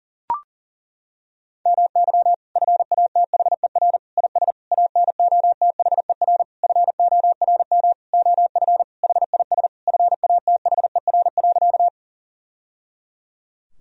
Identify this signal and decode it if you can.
Morse code, heard as a keyed tone, about 37 wpm, 700 Hz. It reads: MY FATHER IS ANOTHER FORM OF HIS FATHER.
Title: Sentences from Top 1000 Words - Review Part 1 - 40wpm